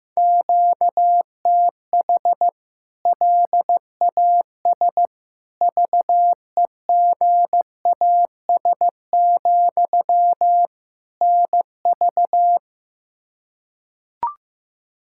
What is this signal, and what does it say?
Morse code, heard as a keyed tone, about 16 wpm, 700 Hz. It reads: QTH LAS VEGAS, NV
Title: K1USN SST QSO Practice 15wpm